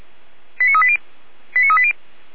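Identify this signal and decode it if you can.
Five-tone selective call (ZVEI), heard as quick runs of five tones, tones of about 70 ms. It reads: 89380 89380
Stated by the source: ZVEI five-tone selective call received on an SDR for Feuerwehr Orlamünde.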